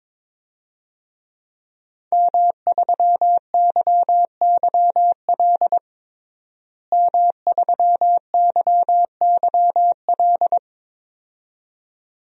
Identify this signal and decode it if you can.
Morse code, heard as a keyed tone, about 22 wpm, 700 Hz. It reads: M3YYL M3YYL
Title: DXCC-100 Call Signs - 5 Characters - Repeated 2x - 22wpm